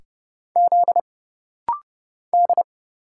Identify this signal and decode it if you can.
Morse code, heard as a keyed tone, about 30 wpm, 700 Hz. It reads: Z D
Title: Single Letter-Number Alt - Rapid-Fire 30wpm